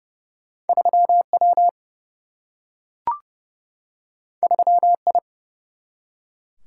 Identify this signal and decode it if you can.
Morse code, heard as a keyed tone, about 32 wpm, 700 Hz. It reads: 3 W 3 I
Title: Single-Letter Single-Number 30wpm